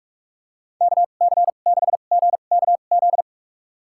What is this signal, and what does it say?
Morse code, heard as a keyed tone, about 44 wpm, spaced as for 32 wpm, 700 Hz. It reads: KC6GKZ
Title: USA Call Signs - 6 Characters - 2x Character Spacing - 45wpm